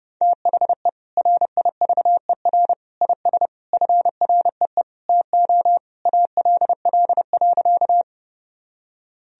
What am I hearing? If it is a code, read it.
Morse code, heard as a keyed tone, about 30 wpm, 700 Hz. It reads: THE RIVER IS FREE TO ALL.
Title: Sentences from Top 1000 Words - Review Part 1 - 30wpm